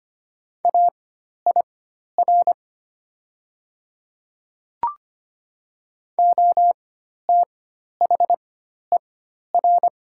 Morse code, heard as a keyed tone, about 25 wpm, spaced as for 12 wpm, 700 Hz. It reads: AIR OTHER
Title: 4x Character Spacing - Top 300 Words - 25wpm